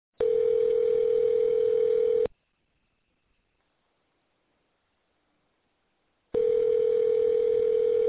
Ringback tone (North America), 440 + 480 Hz.